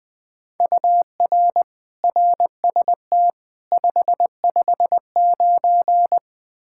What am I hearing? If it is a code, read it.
Morse code, heard as a keyed tone, about 21 wpm, 700 Hz. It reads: UR RST 559
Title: K1USN SST QSO Practice 20wpm